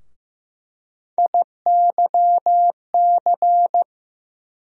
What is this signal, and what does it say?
Morse code, heard as a keyed tone, about 15 wpm, 700 Hz. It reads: IYC
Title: USA Call Sign Postfixes - 3 Letters 15wpm